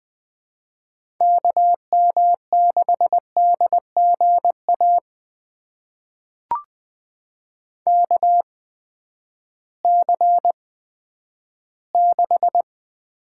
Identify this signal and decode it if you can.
Morse code, heard as a keyed tone, about 20 wpm, spaced as for 11 wpm, 700 Hz. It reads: KM6DGA KC6
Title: USA Call Signs - 6 Characters - 8x Character Spacing - 20wpm